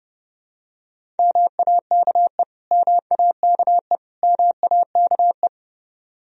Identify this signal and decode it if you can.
Morse code, heard as a keyed tone, about 30 wpm, 700 Hz. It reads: MAKE MAKE MAKE
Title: Top 100 Words Repeated 30wpm